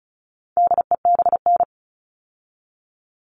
Morse code, heard as a keyed tone, about 34 wpm, 700 Hz. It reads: DEBN
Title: All Cut Numbers - 4-Digit 35wpm